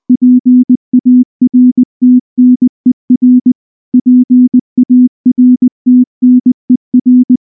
Morse code, heard as a keyed tone, about 19 wpm, 250 Hz. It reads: PARTNER PARTNER
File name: morse / morse-training-20WPM-4